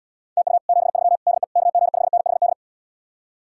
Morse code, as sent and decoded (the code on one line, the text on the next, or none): - ..- / --... ...-- / -.. / -.- --. ..... -. .... -..
TU 73 DE KG5NHD